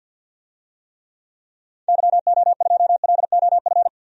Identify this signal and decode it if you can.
Morse code, heard as a keyed tone, about 46 wpm, 700 Hz. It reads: YO2LOF
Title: DXCC-100 Call Signs - 6 Characters - 50wpm